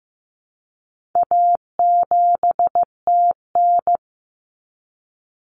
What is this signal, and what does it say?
Morse code, heard as a keyed tone, about 15 wpm, 700 Hz. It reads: A7TN
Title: Common Cut Numbers - 4-Digit 15wpm